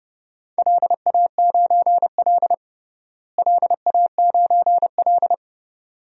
Morse code, heard as a keyed tone, about 32 wpm, 700 Hz. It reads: LA9L LA9L
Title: DXCC-100 Call Signs - 4 Characters - Repeated 2x - 30wpm